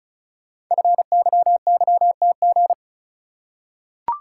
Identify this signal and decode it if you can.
Morse code, heard as a keyed tone, about 36 wpm, 690 Hz. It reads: FYYTG
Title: Code Groups - 5 Letters - 35wpm